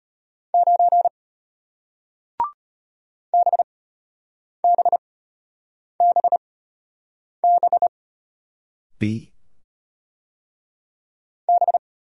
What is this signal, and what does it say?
Morse code, heard as a keyed tone, about 33 wpm, spaced as for 8 wpm, 700 Hz. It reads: 9 BBBB B